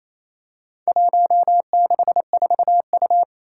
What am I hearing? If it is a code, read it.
Morse code, heard as a keyed tone, about 28 wpm, 700 Hz. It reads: 164U
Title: Four Challenging Letter-Number Combinations 28wpm